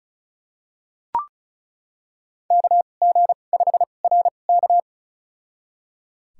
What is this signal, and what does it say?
Morse code, heard as a keyed tone, about 36 wpm, spaced as for 25 wpm, 700 Hz. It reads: KG5RK
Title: USA Call Signs - 5 Characters - 2x Character Spacing - 35wpm